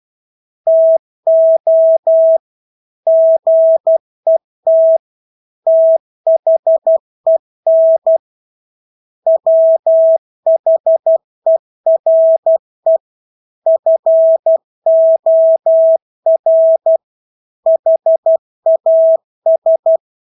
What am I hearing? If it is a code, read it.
Morse code, heard as a keyed tone, about 12 wpm, 650 Hz. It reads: TO GET THEN WHERE FOR HAS
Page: Engelske ord 12 wpm